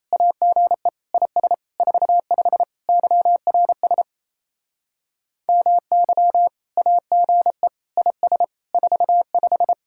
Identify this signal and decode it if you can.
Morse code, heard as a keyed tone, about 36 wpm, 700 Hz. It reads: AGE IS 45 YRS MY AGE IS 45